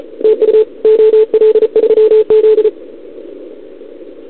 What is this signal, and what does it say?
Morse code, heard as a keyed tone, about 34 wpm, 440 Hz. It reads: TU OL3Z
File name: CQWWCW2021 / Doderna OL3Z